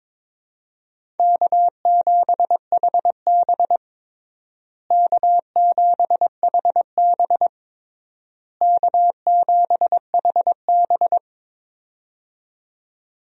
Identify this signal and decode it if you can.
Morse code, heard as a keyed tone, about 22 wpm, 700 Hz. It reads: K7HB K7HB K7HB
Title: USA Call Signs - 4 Characters - Repeated 3x - 22wpm